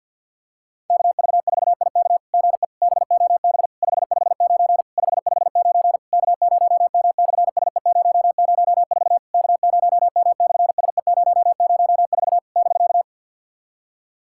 Morse code, heard as a keyed tone, about 50 wpm, 700 Hz. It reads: KV4IK GE BOB 559 559 K0M/SE004 K0M/SE004 <BK>
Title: SOTA QSO Practice 50wpm